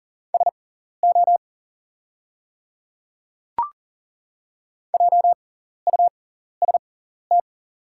Morse code, heard as a keyed tone, about 40 wpm, spaced as for 14 wpm, 700 Hz. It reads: SO JUST
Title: 6x Character Spacing - Top 200 Words - 40wpm